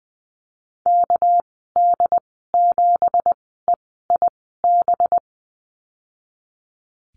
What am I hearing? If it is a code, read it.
Morse code, heard as a keyed tone, about 20 wpm, spaced as for 14 wpm, 700 Hz. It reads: KD7EIB
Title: 238 Lesson 38g - Letter q - Call Signs 20wpm